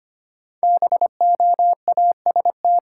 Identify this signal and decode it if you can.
Morse code, heard as a keyed tone, about 25 wpm, 700 Hz. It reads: BOAST
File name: 5 Letter Words - First 3 Letters Given - 25wpm